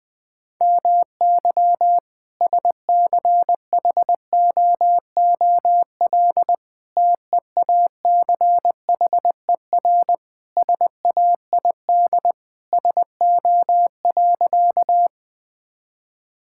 Morse code, heard as a keyed tone, about 20 wpm, 700 Hz. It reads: MY SCHOOL TEACHER SAID SO.
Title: Sentences from Top 1000 Words - Review Part 6 - 20wpm